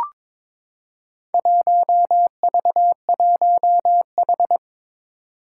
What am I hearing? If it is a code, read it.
Morse code, heard as a keyed tone, about 22 wpm, 700 Hz. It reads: 1V1H
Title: Four Challenging Letter-Number Combinations 22wpm